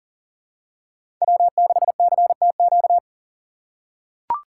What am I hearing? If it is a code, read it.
Morse code, heard as a keyed tone, about 40 wpm, 700 Hz. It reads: W6CTQ
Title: USA Call Signs - 5 Characters - 2x Character Spacing - 40wpm